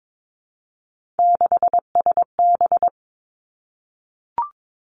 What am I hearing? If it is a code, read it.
Morse code, heard as a keyed tone, about 22 wpm, 700 Hz. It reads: 6SB
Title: Three Challenging Letter-Number Combinations 22wpm